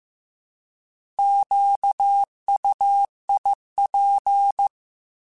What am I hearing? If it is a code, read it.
Morse code, heard as a keyed tone, about 15 wpm, 800 Hz. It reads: QUIP
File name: Alphabetical / Alphabetical mp3s / Qs @13